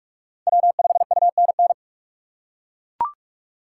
Morse code, heard as a keyed tone, about 42 wpm, 700 Hz. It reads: W5UNN
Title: USA Call Signs - 5 Characters - 6x Character Spacing - 45wpm